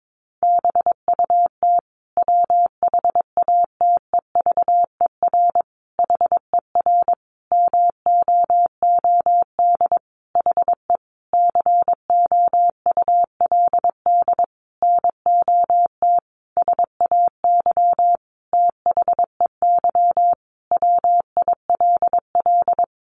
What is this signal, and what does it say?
Morse code, heard as a keyed tone, about 22 wpm, 700 Hz. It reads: BUT WHATEVER HER MOOD HE COULD NOT SAY THEY WILL
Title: Sentences from Top 1100 Words 22wpm